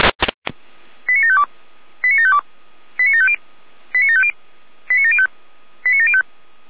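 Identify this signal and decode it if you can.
Five-tone selective call (ZVEI), heard as quick runs of five tones, tones of about 70 ms. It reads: 89752 89752 89750 89750 89705 89705
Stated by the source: ZVEI five-tone selective call 89485 received via SDR.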